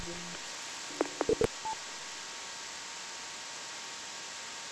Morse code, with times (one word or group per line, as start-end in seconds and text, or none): none